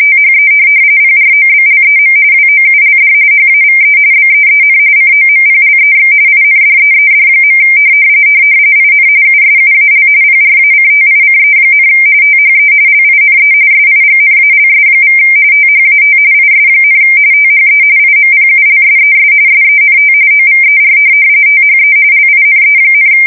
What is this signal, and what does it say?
RTTY, about 45.45 baud. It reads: THE PROBLEM WOULD HAVE BEEN COMPLEX INDEED. IN USUAL PRACTICE, THE LINE SIGNALS= THAT IS, THE IMPULSES SENT OVER THE LINE,
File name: fsk170 / printing-telegraph